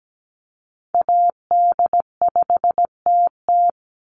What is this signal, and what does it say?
Morse code, heard as a keyed tone, about 17 wpm, 700 Hz. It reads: AD5TT
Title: Call Signs Repeated 3x - 17wpm